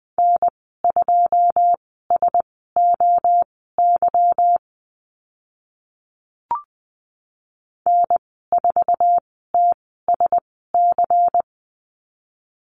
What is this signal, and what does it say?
Morse code, heard as a keyed tone, about 20 wpm, spaced as for 14 wpm, 700 Hz. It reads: N2SOY N4TSC
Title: 167 Lesson 27g - Number 7 - Call Signs 20wpm